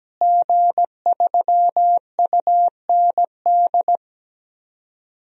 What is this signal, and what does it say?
Morse code, heard as a keyed tone, about 17 wpm, 700 Hz. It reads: G3UND
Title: DXCC-100 Call Signs - 5 Characters - Repeated 3x - 17wpm